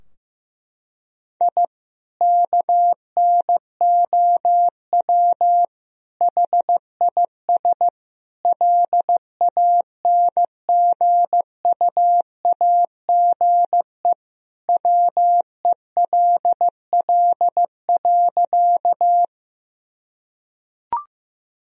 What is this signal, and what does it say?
Morse code, heard as a keyed tone, about 15 wpm, 700 Hz. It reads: I KNOW HIS LANGUAGE WELL.
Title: Sentences from Top 1000 Words - Review Part 3 - 15wpm